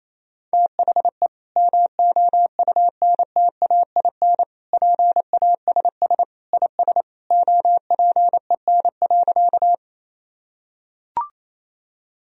Morse code, heard as a keyed tone, about 28 wpm, 700 Hz. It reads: THE MOUNTAIN PASS IS OPEN.
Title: Sentences from Top 1000 Words - Review Part 1 - 28wpm